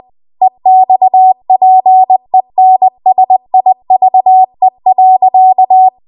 Morse code, heard as a keyed tone, about 20 wpm, 750 Hz. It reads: EXPENSIVE.
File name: Archive / 20 WPM / 20WPM